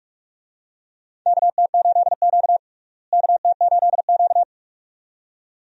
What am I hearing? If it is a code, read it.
Morse code, heard as a keyed tone, about 46 wpm, 700 Hz. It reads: KT8Q KT8Q
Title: USA Call Signs - 4 Characters - Repeated 2x - 45wpm